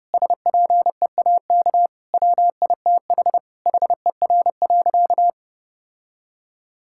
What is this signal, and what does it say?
Morse code, heard as a keyed tone, about 30 wpm, 700 Hz. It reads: SPEAK WITH HER.
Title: Sentences from Top 1600 Words 30wpm